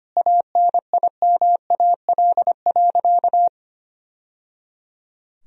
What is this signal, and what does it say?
Morse code, heard as a keyed tone, about 25 wpm, 700 Hz. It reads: ANIMAL.
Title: Sentences from Top 1000 Words - Review Part 1 - 25wpm